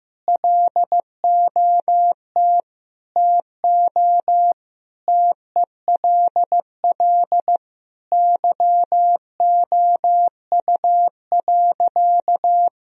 Morse code, heard as a keyed tone, about 15 wpm, 700 Hz. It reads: LOT TO TELL YOU.
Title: Sentences from Top 1000 Words - Review Part 3 - 15wpm